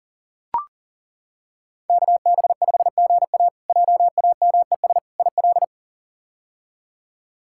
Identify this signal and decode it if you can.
Morse code, heard as a keyed tone, about 40 wpm, 700 Hz. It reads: KB5GA JAMES IL
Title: K1USN SST QSO Practice - 40wpm